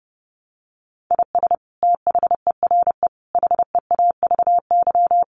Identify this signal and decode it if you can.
Morse code, heard as a keyed tone, about 30 wpm, 700 Hz. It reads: IS THERE HEAVY
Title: Sentences from Top 1300 Words 30wpm